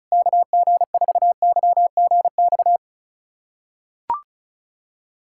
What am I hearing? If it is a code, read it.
Morse code, heard as a keyed tone, about 36 wpm, 700 Hz. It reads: KG4YGX
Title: USA Callsign DE Callsign - 6 Characters - 35wpm